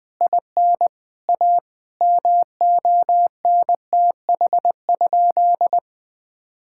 Morse code, heard as a keyed tone, about 20 wpm, 700 Hz. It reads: IN A MONTH?
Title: Sentences from Top 1000 Words - Review Part 6 - 20wpm